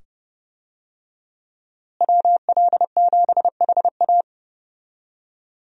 Morse code, heard as a keyed tone, about 30 wpm, 700 Hz. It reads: WL7HA
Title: USA Call Signs - 5 Characters - Repeated 2x - 30wpm